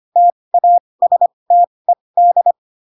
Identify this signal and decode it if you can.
Morse code, heard as a keyed tone, about 25 wpm, spaced as for 20 wpm, 700 Hz. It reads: TASTED